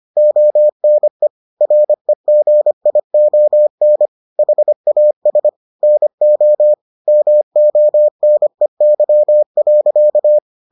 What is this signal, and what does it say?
Morse code, heard as a keyed tone, about 25 wpm, 600 Hz. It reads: ONE REGION HAS NO MONEY.